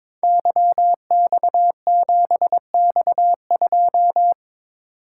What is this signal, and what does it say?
Morse code, heard as a keyed tone, about 22 wpm, 700 Hz. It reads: YX7X2
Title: Code Groups - 5 Letter:Numbers - 22wpm